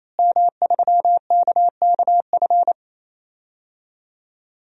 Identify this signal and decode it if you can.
Morse code, heard as a keyed tone, about 27 wpm, 700 Hz. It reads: M3KKF
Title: DXCC-100 Call Signs - 5 Characters - Repeated 3x - 28wpm